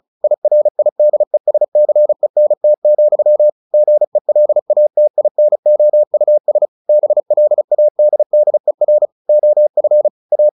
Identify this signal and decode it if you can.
Morse code, heard as a keyed tone, about 35 wpm, 600 Hz. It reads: IRIDESCENT, GELATINOUS BLADDER OF A
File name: the old man and the sea / the old man and the sea 1 35wpm